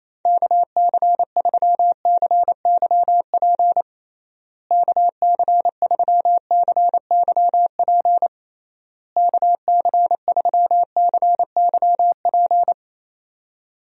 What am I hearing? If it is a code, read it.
Morse code, heard as a keyed tone, about 28 wpm, 700 Hz. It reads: KC3CYP KC3CYP KC3CYP
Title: USA Call Signs - 6 Characters - Repeated 3x - 28wpm